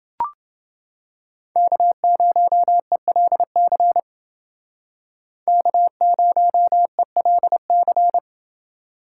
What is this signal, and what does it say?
Morse code, heard as a keyed tone, about 28 wpm, 700 Hz. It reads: K0ELC K0ELC